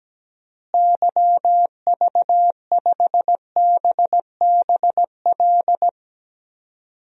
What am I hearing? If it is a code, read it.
Morse code, heard as a keyed tone, about 17 wpm, 700 Hz. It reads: YV5BBL
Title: DXCC-100 Call Signs - 6 Characters - Repeated 3x - 17wpm